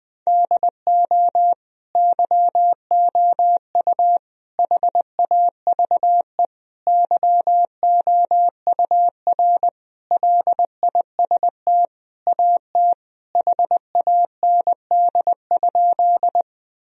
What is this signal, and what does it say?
Morse code, heard as a keyed tone, about 20 wpm, 700 Hz. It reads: DO YOU HAVE YOUR LIST AT HAND?
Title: Sentences from Top 1000 Words - Review Part 2 - 20wpm